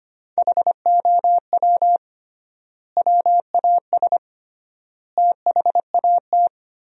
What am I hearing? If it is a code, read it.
Morse code, heard as a keyed tone, about 25 wpm, 700 Hz. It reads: HOW WAS THAT
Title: 3x Word-Spacing - Sentences from Top 500 Words - 25wpm